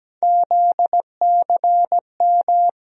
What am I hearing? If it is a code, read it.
Morse code, heard as a keyed tone, about 17 wpm, 700 Hz. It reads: ZCM
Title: USA Call Sign Postfixes - 3 Letters 17wpm